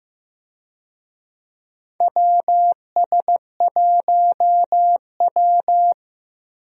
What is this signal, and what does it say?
Morse code, heard as a keyed tone, about 15 wpm, 700 Hz. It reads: WS1W
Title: USA Call Signs - 4 Characters - Repeated 2x - 15wpm